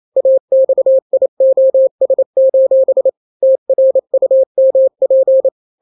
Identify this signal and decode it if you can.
Morse code, heard as a keyed tone, about 28 wpm, 530 Hz. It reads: AXIOS: TRUMP
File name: daily / 27wpm